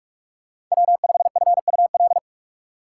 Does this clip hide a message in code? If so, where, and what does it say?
Yes: Morse code from 0.7 to 2.2 s, heard as a keyed tone, about 42 wpm, 700 Hz. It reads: W5FUL